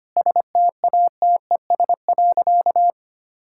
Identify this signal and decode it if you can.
Morse code, heard as a keyed tone, about 25 wpm, 700 Hz. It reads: STATES.
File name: practice / 25wpm / Sentences with 9-Word Phrases - 25wpm